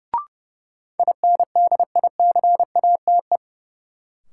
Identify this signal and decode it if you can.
Morse code, heard as a keyed tone, about 30 wpm, 700 Hz. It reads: INDICATE
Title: Sentences from Top 1300 Words - New Word Drill - 30wpm